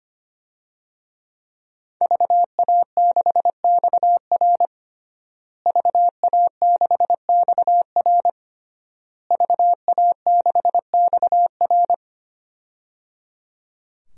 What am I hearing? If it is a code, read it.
Morse code, heard as a keyed tone, about 25 wpm, 700 Hz. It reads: VA6XR VA6XR VA6XR
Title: DXCC-100 Call Signs - 5 Characters - Repeated 3x - 25wpm